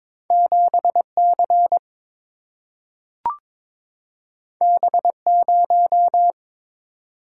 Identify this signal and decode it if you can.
Morse code, heard as a keyed tone, about 22 wpm, 700 Hz. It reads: 7 C B 0
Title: Single-Letter Single-Number 22wpm